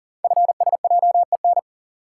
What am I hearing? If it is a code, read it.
Morse code, heard as a keyed tone, about 39 wpm, 700 Hz. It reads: FSJEN
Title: Code Groups - 5 Letters - 40wpm